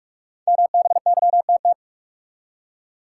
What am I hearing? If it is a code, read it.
Morse code, heard as a keyed tone, about 46 wpm, 700 Hz. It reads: MBYTT
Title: Code Groups - 5 Letter:Numbers - 45wpm